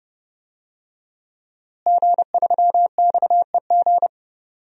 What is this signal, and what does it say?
Morse code, heard as a keyed tone, about 30 wpm, 700 Hz. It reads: G3XEG
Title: DXCC-100 Call Signs - 5 Characters - Repeated 3x - 30wpm